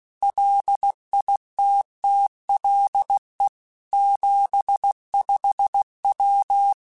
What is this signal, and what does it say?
Morse code, heard as a keyed tone, about 16 wpm, 800 Hz. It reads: LITTLE 75W